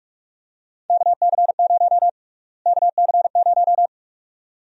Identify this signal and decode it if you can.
Morse code, heard as a keyed tone, about 46 wpm, 700 Hz. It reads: KC0 KC0
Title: USA Call Sign Prefixes - Repeated 2x - 45wpm